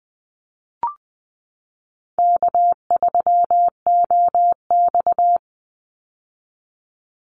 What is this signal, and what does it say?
Morse code, heard as a keyed tone, about 19 wpm, 700 Hz. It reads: K3OX
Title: Easy Call Signs - 20wpm